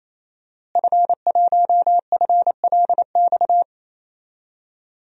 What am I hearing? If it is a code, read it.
Morse code, heard as a keyed tone, about 28 wpm, 700 Hz. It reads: F1FLX
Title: DXCC-100 Call Signs - 5 Characters - Repeated 3x - 28wpm